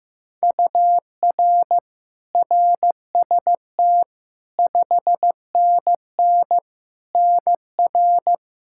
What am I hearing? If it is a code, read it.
Morse code, heard as a keyed tone, about 15 wpm, 700 Hz. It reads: UR RST 5NN NR
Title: Practice QSO - Exchange 1 - Beginning - 15wpm